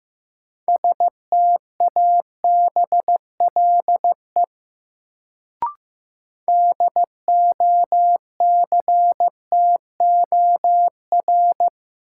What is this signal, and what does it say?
Morse code, heard as a keyed tone, about 15 wpm, 700 Hz. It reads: STABLE DOCTOR
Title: Sentences from Top 1500 Words - New Word Drill Alt - 15wpm